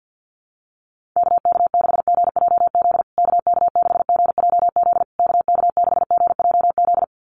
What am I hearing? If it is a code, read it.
Morse code, heard as a keyed tone, about 48 wpm, 700 Hz. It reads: KK6GJZ KK6GJZ KK6GJZ
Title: USA Call Signs Repeated 3x 50wpm